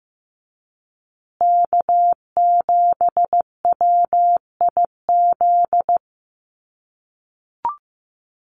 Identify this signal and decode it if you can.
Morse code, heard as a keyed tone, about 15 wpm, 700 Hz. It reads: K7WIZ
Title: USA Call Signs Repeated 2x 15wpm